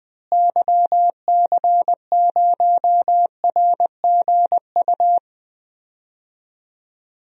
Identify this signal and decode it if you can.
Morse code, heard as a keyed tone, about 20 wpm, 700 Hz. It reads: YC0RGU
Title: DXCC-100 Call Signs - 6 Characters - Repeated 2x - 20wpm